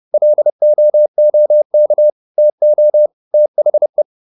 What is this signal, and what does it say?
Morse code, heard as a keyed tone, about 30 wpm, 600 Hz. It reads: LOOK TO THE